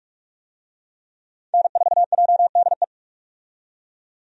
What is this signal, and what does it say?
Morse code, heard as a keyed tone, about 41 wpm, 700 Hz. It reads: N4JDE
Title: USA Call Signs - 5 Characters - 45wpm